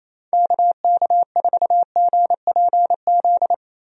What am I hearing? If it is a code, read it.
Morse code, heard as a keyed tone, about 27 wpm, 700 Hz. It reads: KK4GPZ
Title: USA Call Signs - 6 Characters - 28wpm